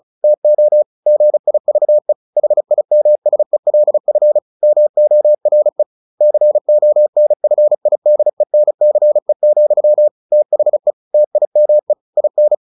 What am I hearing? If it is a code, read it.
Morse code, heard as a keyed tone, about 37 wpm, 600 Hz. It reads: TO GIVE HIMSELF MORE CONFIDENCE, THE TIME IN
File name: the old man and the sea / the old man and the sea 1 35wpm